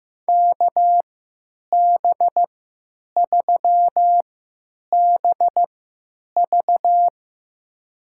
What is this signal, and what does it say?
Morse code, heard as a keyed tone, about 15 wpm, spaced as for 9 wpm, 700 Hz. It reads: KB3BV
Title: USA Call Signs - 5 Characters - 3x Character Spacing - 15wpm